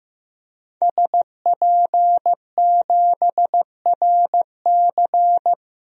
Morse code, heard as a keyed tone, about 15 wpm, 700 Hz. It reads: SP7RC
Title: DXCC-100 Call Signs - 5 Characters - Repeated 2x - 15wpm